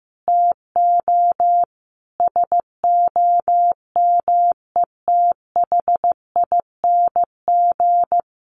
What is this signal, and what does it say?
Morse code, heard as a keyed tone, about 15 wpm, 700 Hz. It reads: TO SOMETHING
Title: Sentences from Top 1100 Words 15wpm